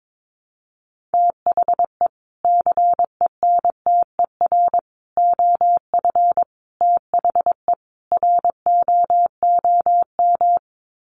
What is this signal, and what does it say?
Morse code, heard as a keyed tone, about 22 wpm, 700 Hz. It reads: THE CENTER OF THE ROOM
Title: Sets of 5 Words Encore 22wpm